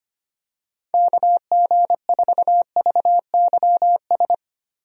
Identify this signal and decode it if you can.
Morse code, heard as a keyed tone, about 25 wpm, 700 Hz. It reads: KG4VYS
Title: USA Call Signs - 6 Characters - Repeated 2x - 25wpm